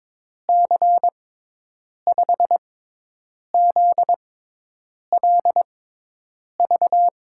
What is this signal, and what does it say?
Morse code, heard as a keyed tone, about 22 wpm, spaced as for 8 wpm, 700 Hz. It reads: C5ZLV
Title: USA Call Signs - 6 Characters - 6x Character Spacing - 22wpm